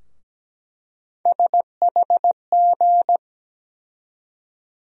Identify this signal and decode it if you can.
Morse code, heard as a keyed tone, about 17 wpm, 700 Hz. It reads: SHG
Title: USA Call Sign Postfixes - 3 Letters 17wpm